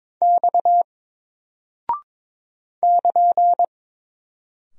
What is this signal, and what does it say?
Morse code, heard as a keyed tone, about 22 wpm, 700 Hz. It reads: X <KN>
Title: Single Character (Letter:Number:Punctuation:Prosign) 22wpm